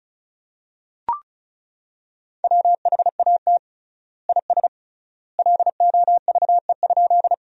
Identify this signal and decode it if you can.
Morse code, heard as a keyed tone, about 36 wpm, 700 Hz. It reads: WHAT IS LOVE?
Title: 3x Word-Spacing - Sentences from Top 400 Words - 35wpm